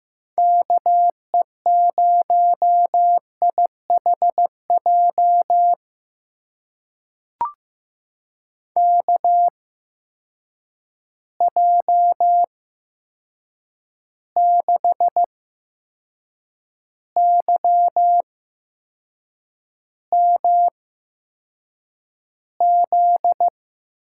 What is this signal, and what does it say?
Morse code, heard as a keyed tone, about 15 wpm, spaced as for 6 wpm, 700 Hz. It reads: KE0IHJ KJ6YMZ
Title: USA Call Signs - 6 Characters - 8x Character Spacing - 15wpm